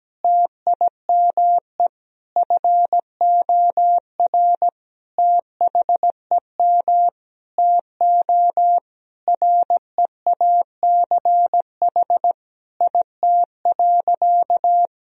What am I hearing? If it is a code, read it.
Morse code, heard as a keyed tone, about 17 wpm, 700 Hz. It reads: TIME FOR THEM TO REACH IT.
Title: Sentences from Top 1000 Words - Review Part 2 - 17wpm